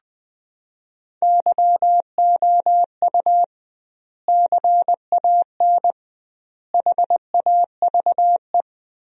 Morse code, heard as a keyed tone, about 20 wpm, 700 Hz. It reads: YOU CAN HAVE
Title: 2x Word-Spacing - Sentences from Top 100 Words - 20wpm